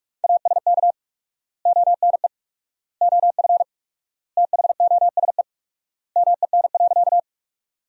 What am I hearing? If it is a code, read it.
Morse code, heard as a keyed tone, about 46 wpm, 700 Hz. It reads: ASK ONE OF THOSE MEN.
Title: 4x Word-Spacing - Sentences from Top 300 Words - 45wpm